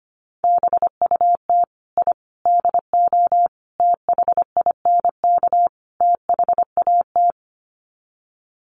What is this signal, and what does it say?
Morse code, heard as a keyed tone, about 25 wpm, 700 Hz. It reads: BUT I DO THINK THAT
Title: Sets of 5 Words Encore2 25wpm